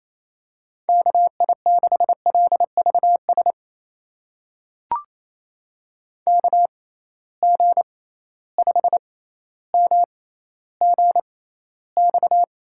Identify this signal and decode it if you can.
Morse code, heard as a keyed tone, about 28 wpm, spaced as for 14 wpm, 700 Hz. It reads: KI6LVS KG5MGX
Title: USA Call Signs - 6 Characters - 6x Character Spacing - 28wpm